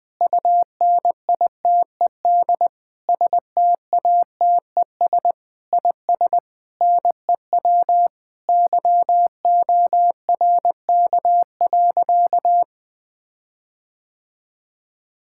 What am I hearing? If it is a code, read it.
Morse code, heard as a keyed tone, about 20 wpm, 700 Hz. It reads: UNITED STATES IS NEW YORK.
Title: Sentences with 5-Word Phrases - 20wpm